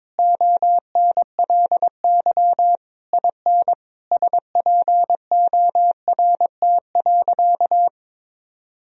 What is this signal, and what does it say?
Morse code, heard as a keyed tone, about 21 wpm, 700 Hz. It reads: ONLY IN SPORT.
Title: Sentences from Top 1000 Words - Review Part 4 - 22wpm